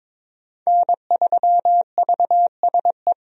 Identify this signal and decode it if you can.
Morse code, heard as a keyed tone, about 22 wpm, 700 Hz. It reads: N3VSE
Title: USA Call Signs - 5 Characters - 22wpm